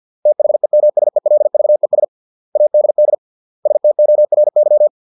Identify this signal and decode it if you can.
Morse code, heard as a keyed tone, about 50 wpm, 600 Hz. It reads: THEMSELVES ADD STORY